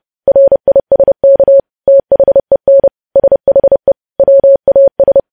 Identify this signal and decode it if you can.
Morse code, heard as a keyed tone, about 30 wpm, 550 Hz. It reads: RISK THEN SHE WAS
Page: news headlines in morse code at 30 wpm